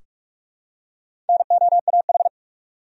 Morse code, heard as a keyed tone, about 42 wpm, 700 Hz. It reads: NOAH
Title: USA Common Names - Max 4 Letters - 45wpm